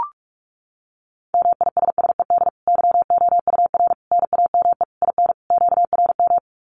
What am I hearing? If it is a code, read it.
Morse code, heard as a keyed tone, about 44 wpm, 700 Hz. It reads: MISSED YOUR NAME IN QRM
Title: Common QSO Phrases 45wpm